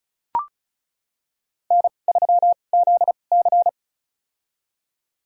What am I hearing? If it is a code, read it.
Morse code, heard as a keyed tone, about 36 wpm, spaced as for 25 wpm, 700 Hz. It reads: N3ZC
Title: USA Call Signs - 4 Characters - 2x Character Spacing - 35wpm